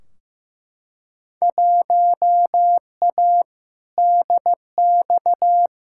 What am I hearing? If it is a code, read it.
Morse code, heard as a keyed tone, about 15 wpm, 700 Hz. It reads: 1A DX
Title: Field Day Exchanges Encore - 15wpm